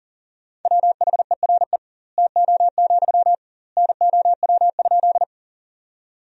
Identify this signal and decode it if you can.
Morse code, heard as a keyed tone, about 37 wpm, 700 Hz. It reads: WHERE TO, NOW?
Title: 2x Word-Spacing - Sentences from Top 200 Words - 40wpm